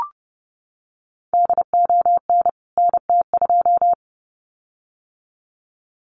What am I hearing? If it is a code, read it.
Morse code, heard as a keyed tone, about 30 wpm, 700 Hz. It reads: DON NT2
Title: CWT Exchange 30wpm